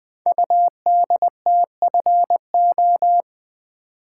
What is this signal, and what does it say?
Morse code, heard as a keyed tone, about 20 wpm, 700 Hz. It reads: UDTFO
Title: Code Groups - 5 Letters - 20wpm